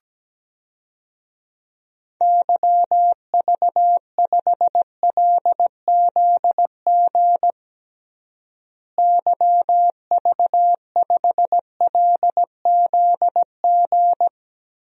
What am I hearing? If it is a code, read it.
Morse code, heard as a keyed tone, about 17 wpm, 700 Hz. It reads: YV5LZG YV5LZG
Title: DXCC-100 Call Signs - 6 Characters - Repeated 3x - 17wpm